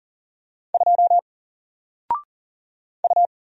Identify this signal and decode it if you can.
Morse code, heard as a keyed tone, about 40 wpm, 700 Hz. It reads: 2 U
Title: Single Letter-Number Alt 40wpm